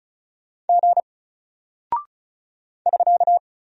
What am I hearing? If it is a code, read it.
Morse code, heard as a keyed tone, about 36 wpm, 700 Hz. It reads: G <SK>
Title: Single Character (Letter:Number:Punctuation:Prosign) 35wpm